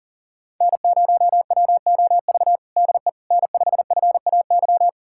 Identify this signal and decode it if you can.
Morse code, heard as a keyed tone, about 40 wpm, 700 Hz. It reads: N0WOV DE N5FAY
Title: USA Callsign DE Callsign - 5 Characters - 40wpm